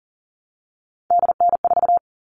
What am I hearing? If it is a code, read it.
Morse code, heard as a keyed tone, about 40 wpm, 700 Hz. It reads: DN4
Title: All Cut Numbers - 3-Digit 40wpm